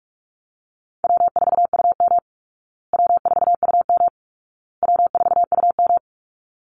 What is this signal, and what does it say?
Morse code, heard as a keyed tone, about 44 wpm, 700 Hz. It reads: W4UM W4UM W4UM
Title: Easy Call Signs Repeated 3x - 45wpm